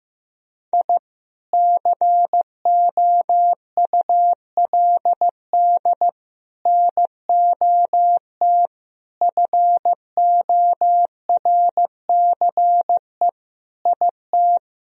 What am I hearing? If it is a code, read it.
Morse code, heard as a keyed tone, about 15 wpm, 700 Hz. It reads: I COULD NOT FORCE IT
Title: Sentences from Top 1000 Words - Review Part 3 - 15wpm